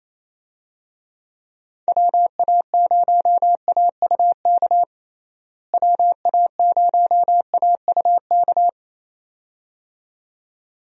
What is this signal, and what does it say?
Morse code, heard as a keyed tone, about 28 wpm, 700 Hz. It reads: WA0AUK WA0AUK
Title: USA Call Signs - 6 Characters - Repeated 2x - 28wpm